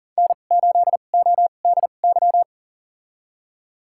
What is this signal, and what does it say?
Morse code, heard as a keyed tone, about 38 wpm, spaced as for 28 wpm, 700 Hz. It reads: N8ODY
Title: USA Call Signs - 5 Characters - 2x Character Spacing - 40wpm